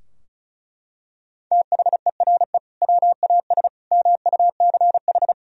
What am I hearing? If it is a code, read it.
Morse code, heard as a keyed tone, about 36 wpm, 700 Hz. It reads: THERE WAS MUCH
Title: Sentences from Top 1000 Words - Review Part 4 - 35wpm